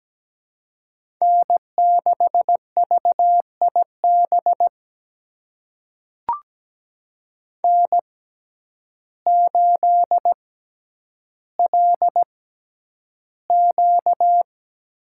Morse code, heard as a keyed tone, about 17 wpm, spaced as for 9 wpm, 700 Hz. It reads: N6VIB N8LQ
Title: USA Call Signs - 5 Characters - 6x Character Spacing - 17wpm